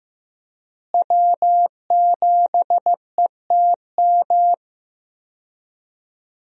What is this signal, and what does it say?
Morse code, heard as a keyed tone, about 15 wpm, 700 Hz. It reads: W7ETM
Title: USA Call Signs - 5 Characters - Repeated 2x - 15wpm